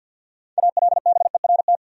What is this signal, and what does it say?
Morse code, heard as a keyed tone, about 52 wpm, 700 Hz. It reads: ALBERT
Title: USA Common Names - 50wpm